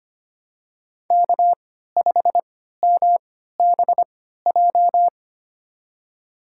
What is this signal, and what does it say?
Morse code, heard as a keyed tone, about 25 wpm, spaced as for 14 wpm, 700 Hz. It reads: K5MBJ
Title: USA Call Signs - 5 Characters - 3x Character Spacing - 25wpm